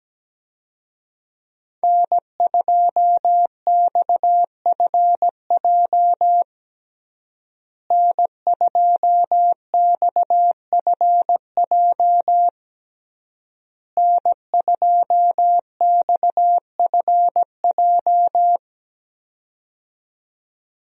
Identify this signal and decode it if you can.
Morse code, heard as a keyed tone, about 17 wpm, 700 Hz. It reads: N2XFJ N2XFJ N2XFJ
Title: USA Call Signs - 5 Characters - Repeated 3x - 17wpm